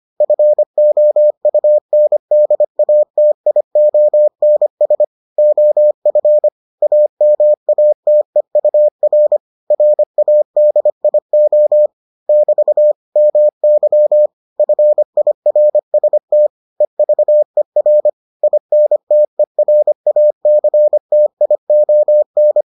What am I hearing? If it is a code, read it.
Morse code, heard as a keyed tone, about 25 wpm, 600 Hz. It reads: FOUNDATIONS OF AMATEUR RADIO <BT> MY FIRST EVER INTERACTION